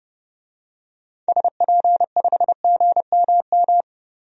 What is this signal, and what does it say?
Morse code, heard as a keyed tone, about 30 wpm, 700 Hz. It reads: SP5GMM
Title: DXCC-100 Call Signs - 6 Characters - Repeated 2x - 30wpm